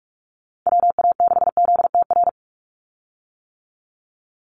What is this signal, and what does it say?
Morse code, heard as a keyed tone, about 46 wpm, 690 Hz. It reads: WA6ZTR
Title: USA Call Signs 45wpm